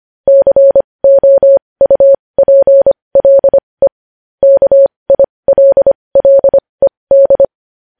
Morse code, heard as a keyed tone, about 25 wpm, spaced as for 20 wpm, 550 Hz. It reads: COUPLE KILLED
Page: news-headlines-in-morse-code-at-20-wpm